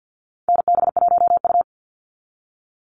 Morse code, heard as a keyed tone, about 50 wpm, 700 Hz. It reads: NB1U
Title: Call Signs Repeated 2x - 50wpm